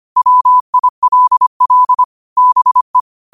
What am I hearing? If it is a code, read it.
Morse code, heard as a keyed tone, about 25 wpm, 1000 Hz. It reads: WILL BE
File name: qotdfiles / 25wpm